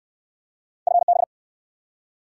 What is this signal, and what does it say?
Morse code, heard as a keyed tone, about 64 wpm, 700 Hz. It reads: FB